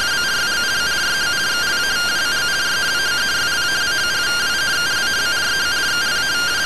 Two-tone teleprinter signal, about 170 Hz shift.